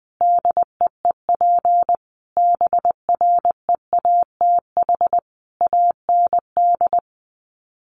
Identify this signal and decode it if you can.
Morse code, heard as a keyed tone, about 20 wpm, 700 Hz. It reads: DEEP BREATH AND
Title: Sets of 5 Words Encore 20wpm